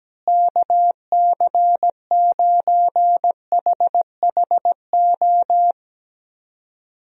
Morse code, heard as a keyed tone, about 17 wpm, 700 Hz. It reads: KC9HHO
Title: USA Call Signs - 6 Characters - Repeated 2x - 17wpm